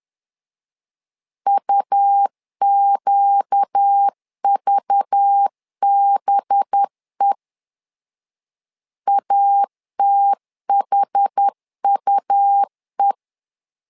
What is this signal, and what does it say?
Morse code, heard as a keyed tone, about 10 wpm, 800 Hz. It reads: UQVBE ATHUE
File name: LEZIONI / 5 LBVC / 5 LBV 10wpm 2248